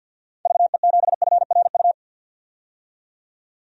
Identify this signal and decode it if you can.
Morse code, heard as a keyed tone, about 52 wpm, 700 Hz. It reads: VE7FRU
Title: DXCC-100 Call Signs - 6 Characters - 50wpm